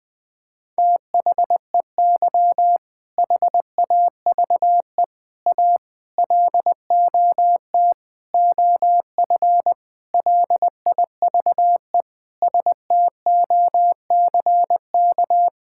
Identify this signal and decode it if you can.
Morse code, heard as a keyed tone, about 20 wpm, 700 Hz. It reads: THEY HAVE A LOT OF LIVE STOCK
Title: Sentences from Top 1000 Words - Review Part 5 - 20wpm